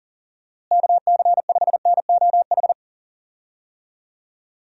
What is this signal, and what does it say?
Morse code, heard as a keyed tone, about 40 wpm, 700 Hz. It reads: KC5NOH
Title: USA Call Signs - 6 Characters - Repeated 3x - 40wpm